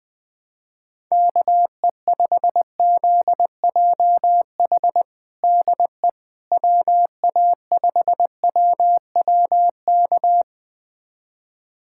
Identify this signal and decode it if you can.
Morse code, heard as a keyed tone, about 21 wpm, 700 Hz. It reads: KE5ZJH DE WA5WWK
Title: USA Callsign DE Callsign - 6 Characters - 20wpm